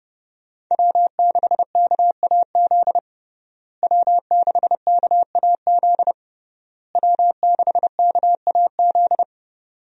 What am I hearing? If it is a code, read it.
Morse code, heard as a keyed tone, about 32 wpm, 700 Hz. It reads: W6KAZ W6KAZ W6KAZ
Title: USA Call Signs - 5 Characters - Repeated 3x - 30wpm